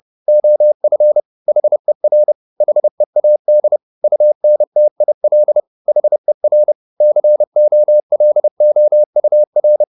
Morse code, heard as a keyed tone, about 30 wpm, 600 Hz. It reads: OF HER HEAD UNTIL HER COLOUR
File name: the old man and the sea / the old man and the sea 1 30wpm